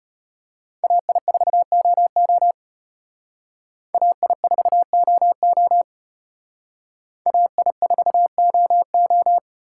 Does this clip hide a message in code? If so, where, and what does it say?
Yes: Morse code from 0.8 to 9.4 s, heard as a keyed tone, about 33 wpm, 700 Hz. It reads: AI4OO AI4OO AI4OO